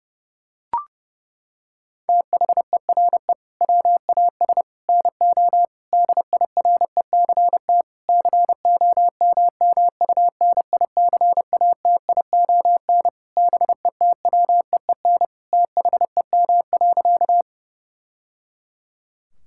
Morse code, heard as a keyed tone, about 32 wpm, 700 Hz. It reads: THERE WAS NO DIRECT COMMUNICATION BETWEEN THEM.
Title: Sentences from Top 1000 Words - Review Part 6 - 30wpm